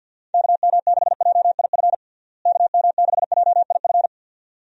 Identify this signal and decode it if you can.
Morse code, heard as a keyed tone, about 50 wpm, 700 Hz. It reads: KM6JIF KM6JIF
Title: USA Call Signs - 6 Characters - Repeated 2x - 50wpm